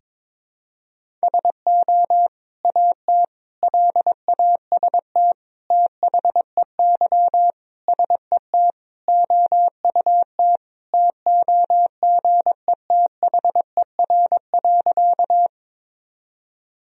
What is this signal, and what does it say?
Morse code, heard as a keyed tone, about 22 wpm, 700 Hz. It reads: SO AT LAST THEY SET OUT TOGETHER.
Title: Sentences from Top 1000 Words - Review Part 1 - 22wpm